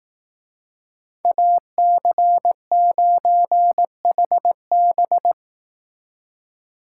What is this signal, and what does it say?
Morse code, heard as a keyed tone, about 18 wpm, 700 Hz. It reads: AC9HB
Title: Speed Racing - USA 2x2 Call Signs 15wpm